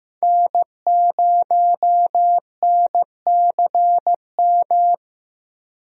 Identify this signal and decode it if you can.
Morse code, heard as a keyed tone, about 15 wpm, 700 Hz. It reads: N0NCM
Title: USA Call Signs - 5 Characters - Repeated 3x - 15wpm